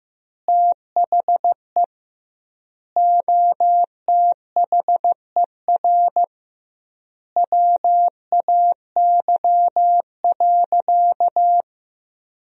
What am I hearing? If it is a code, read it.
Morse code, heard as a keyed tone, about 15 wpm, 700 Hz. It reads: THE OTHER WAY.
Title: 2x Word-Spacing - Sentences from Top 100 Words - 15wpm